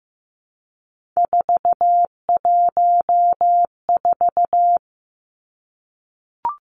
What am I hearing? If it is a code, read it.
Morse code, heard as a keyed tone, about 15 wpm, 700 Hz. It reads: 414
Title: Numbers - 3-Digit 15wpm